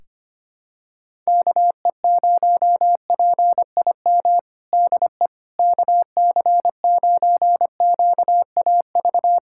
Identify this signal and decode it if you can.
Morse code, heard as a keyed tone, about 25 wpm, 700 Hz. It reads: KE0PIM DE KC9QAV
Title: USA Callsign DE Callsign - 6 Characters - 25wpm